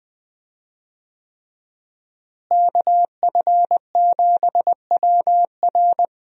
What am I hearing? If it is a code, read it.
Morse code, heard as a keyed tone, about 20 wpm, 700 Hz. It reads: KF7WR
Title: USA Call Signs - 5 Characters - Repeated 3x - 20wpm